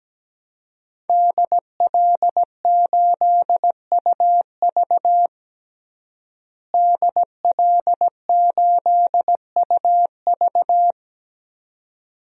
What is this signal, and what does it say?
Morse code, heard as a keyed tone, about 17 wpm, 700 Hz. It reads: DL8UV DL8UV
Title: DXCC-100 Call Signs - 5 Characters - Repeated 2x - 17wpm